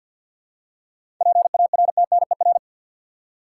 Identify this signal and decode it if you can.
Morse code, heard as a keyed tone, about 53 wpm, 700 Hz. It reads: PARTNER
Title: Seven Letter Words - 50wpm